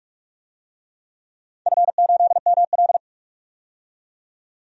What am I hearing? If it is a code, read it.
Morse code, heard as a keyed tone, about 46 wpm, 700 Hz. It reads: F8ML
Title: DXCC-100 Call Signs - 4 Characters - 45wpm